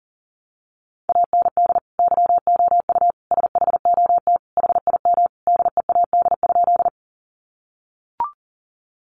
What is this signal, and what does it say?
Morse code, heard as a keyed tone, about 40 wpm, 700 Hz. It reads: AND YOU SHOT HIM DEAD?
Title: Sentences from Top 1500 Words 40wpm